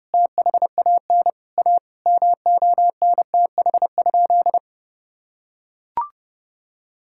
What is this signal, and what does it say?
Morse code, heard as a keyed tone, about 32 wpm, 700 Hz. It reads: THAN A MONTH?
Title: Sentences from Top 1000 Words - Review Part 5 - 30wpm